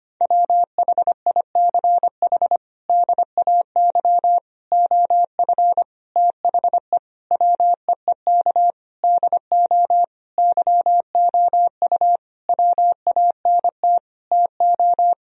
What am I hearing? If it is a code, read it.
Morse code, heard as a keyed tone, about 25 wpm, 700 Hz. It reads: WHICH DAY OF THE WEEK DO YOU WANT TO
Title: Sentences from Top 1000 Words - Review Part 3 - 25wpm